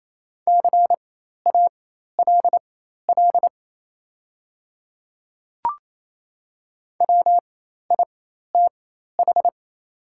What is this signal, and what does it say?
Morse code, heard as a keyed tone, about 28 wpm, spaced as for 13 wpm, 700 Hz. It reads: CALL WITH
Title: 4x Character Spacing - Top 100 Words - 28wpm